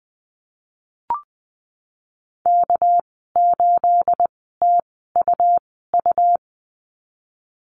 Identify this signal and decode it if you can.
Morse code, heard as a keyed tone, about 20 wpm, spaced as for 15 wpm, 700 Hz. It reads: K8TUU
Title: 213 Lesson 34g - Number 8 - Call Signs 20wpm